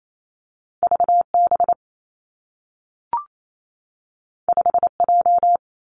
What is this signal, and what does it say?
Morse code, heard as a keyed tone, about 28 wpm, 700 Hz. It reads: V B 5 J
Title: Two Challenging Letter-Number Combinations 28wpm